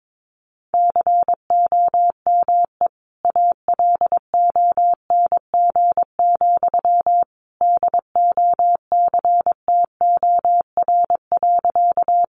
Morse code, heard as a keyed tone, about 22 wpm, 700 Hz. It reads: COME ALONG, DOCTOR.
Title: Sentences from Top 1500 Words 22wpm